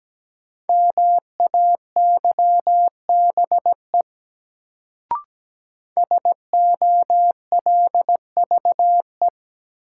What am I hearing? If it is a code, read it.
Morse code, heard as a keyed tone, about 17 wpm, 700 Hz. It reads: MAYBE SOLVE
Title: Sentences from Top 1000 Words - New Word Drill - 17wpm